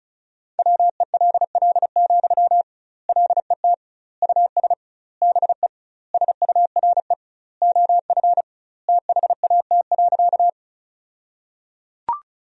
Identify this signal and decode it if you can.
Morse code, heard as a keyed tone, about 36 wpm, 700 Hz. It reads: WELL, LET US BE SURE OF THAT.
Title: 2x Word-Spacing - Sentences from Top 300 Words - 35wpm